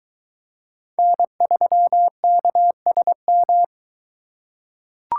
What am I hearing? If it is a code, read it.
Morse code, heard as a keyed tone, about 22 wpm, 700 Hz. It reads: N3KSM